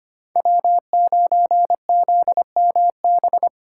Morse code, heard as a keyed tone, about 25 wpm, 700 Hz. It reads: W9ZMB
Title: USA Call Signs - 5 Characters - 6x Character Spacing - 25wpm